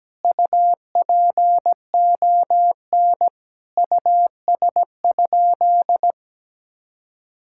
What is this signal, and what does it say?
Morse code, heard as a keyed tone, about 17 wpm, 700 Hz. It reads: UPON US?
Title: Sentences from Top 1600 Words 17wpm